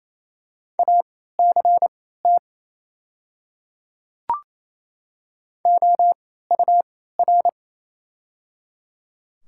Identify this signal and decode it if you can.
Morse code, heard as a keyed tone, about 28 wpm, spaced as for 16 wpm, 700 Hz. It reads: ACT OUR
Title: 3x Character Spacing - Top 300 Words - 28wpm